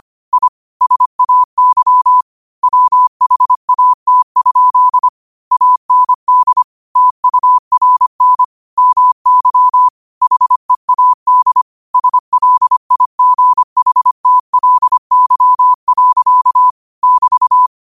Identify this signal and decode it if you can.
Morse code, heard as a keyed tone, about 25 wpm, 1000 Hz. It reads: I SAY WHAT? AND TURN MY HEAD SLIGHTLY. <BT>